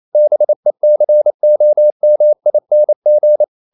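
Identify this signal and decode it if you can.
Morse code, heard as a keyed tone, about 28 wpm, 600 Hz. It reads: BECOMING
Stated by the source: Morse code practice at 28 wpm, text of Ken Follett's novel Never in English, from CW med Gnister.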